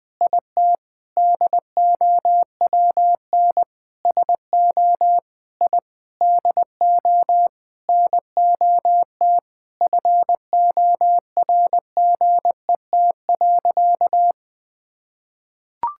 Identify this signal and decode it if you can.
Morse code, heard as a keyed tone, about 20 wpm, 700 Hz. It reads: IT DOWN SO I DO NOT FORGET.
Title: Sentences with 4-Word Phrases - 20wpm